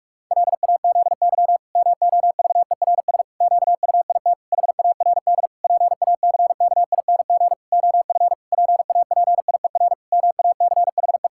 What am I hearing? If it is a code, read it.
Morse code, heard as a keyed tone, about 46 wpm, 700 Hz. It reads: LAZY MOVERS QUIT HARD PACKING OF PAPIER MACHE
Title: Pangram Challenge v2 - 45wpm